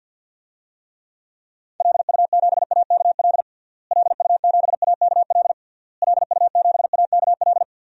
Morse code, heard as a keyed tone, about 49 wpm, 700 Hz. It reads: LU7AKL LU7AKL LU7AKL
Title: DXCC-100 Call Signs - 6 Characters - Repeated 3x - 50wpm